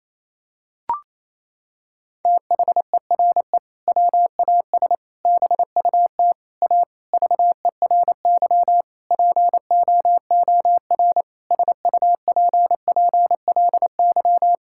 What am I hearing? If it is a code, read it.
Morse code, heard as a keyed tone, about 28 wpm, 700 Hz. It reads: THERE WAS BUT A VERY POOR SUPPLY
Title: Sentences from Top 1000 Words - Review Part 5 - 28wpm